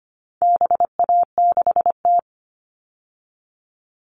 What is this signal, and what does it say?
Morse code, heard as a keyed tone, about 25 wpm, 700 Hz. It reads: BA6T
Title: All Cut Numbers - 4-Digit 25wpm